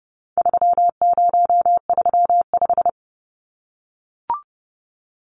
Morse code, heard as a keyed tone, about 28 wpm, 700 Hz. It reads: 3035
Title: Numbers - 4-Digit 30wpm